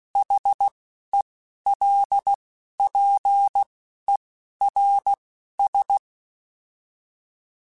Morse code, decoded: HELPERS

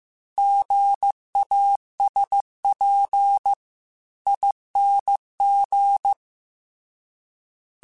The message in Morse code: GASPING

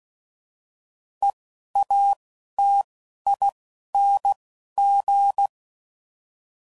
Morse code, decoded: E A T I N G

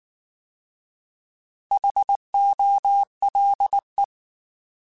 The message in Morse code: HOLE

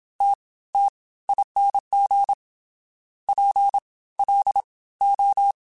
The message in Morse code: TTING PLO

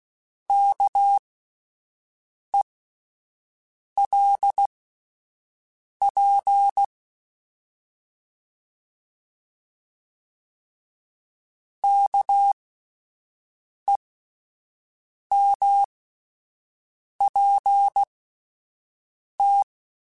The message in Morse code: KELP KEMPT

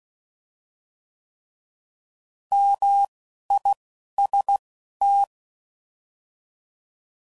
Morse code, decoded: MIST

